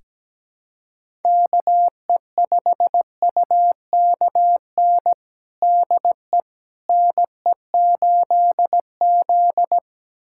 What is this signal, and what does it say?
Morse code, heard as a keyed tone, about 17 wpm, 700 Hz. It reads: KE5UKN DE NE8Z
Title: USA Callsign DE Callsign 17wpm